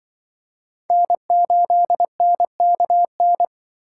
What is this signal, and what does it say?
Morse code, heard as a keyed tone, about 25 wpm, 700 Hz. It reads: N8NKN
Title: Speed Racing - USA 1x3 Call Signs 20wpm